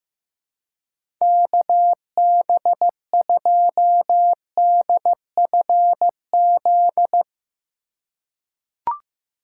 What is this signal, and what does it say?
Morse code, heard as a keyed tone, about 15 wpm, 700 Hz. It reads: KB2DFZ
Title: USA Call Signs - 4x Character Spacing - 15wpm